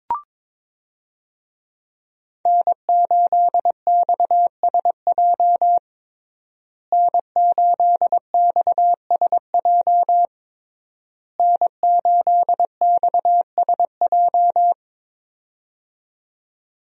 Morse code, heard as a keyed tone, about 22 wpm, 700 Hz. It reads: N8XSJ N8XSJ N8XSJ
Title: USA Call Signs - 5 Characters - Repeated 3x - 22wpm